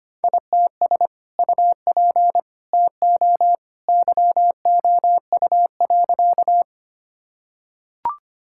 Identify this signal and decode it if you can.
Morse code, heard as a keyed tone, about 25 wpm, 700 Hz. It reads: ITS UP TO YOU.
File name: practice / 25wpm / Sentences with 2-Word Phrases - 25wpm